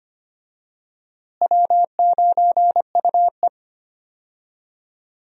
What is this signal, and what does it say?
Morse code, heard as a keyed tone, about 25 wpm, 700 Hz. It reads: W9UE
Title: USA Call Signs - 4 Characters - Repeated 3x - 25wpm